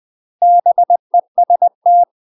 Morse code, heard as a keyed tone, about 20 wpm, 700 Hz. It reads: BEST
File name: UK MASTER QSOs TRAINING / 20WPM / UK MASTER QSOs 0062